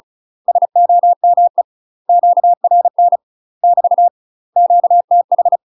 Morse code, heard as a keyed tone, about 36 wpm, 700 Hz. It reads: SOME QRN <BT> QTH